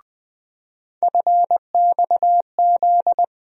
Morse code, heard as a keyed tone, about 20 wpm, 700 Hz. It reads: FXZ
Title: USA Call Sign Postfixes - 3 Letters 20wpm